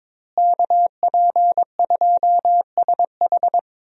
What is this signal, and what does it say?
Morse code, heard as a keyed tone, about 22 wpm, 700 Hz. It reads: KP2SH
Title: DXCC-100 Call Signs - 5 Characters - Repeated 2x - 22wpm